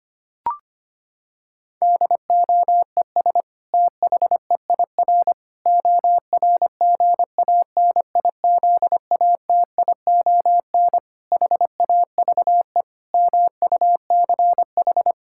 Morse code, heard as a keyed tone, about 25 wpm, 700 Hz. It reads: DOES THEIR ORGANIZATION HAVE MUCH
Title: Sentences from Top 1000 Words - Review Part 5 - 25wpm